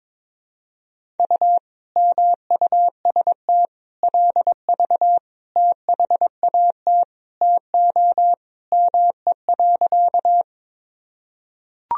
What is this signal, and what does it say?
Morse code, heard as a keyed tone, about 22 wpm, 700 Hz. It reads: U MUST LV THAT TO ME.
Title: Sentences from Top 400 Words and Abbreviations 22wpm